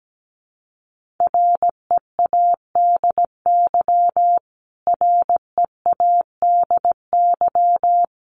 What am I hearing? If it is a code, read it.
Morse code, heard as a keyed tone, about 17 wpm, 700 Hz. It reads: READY READY
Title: Top 500 Words Repeated 17wpm